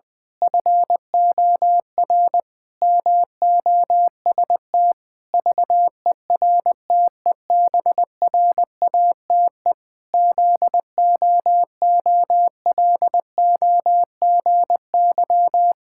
Morse code, heard as a keyed tone, about 20 wpm, 700 Hz. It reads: FOR MOST VERTEBRATE ZOOLOGY
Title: Pangram Challenge v2 - 20wpm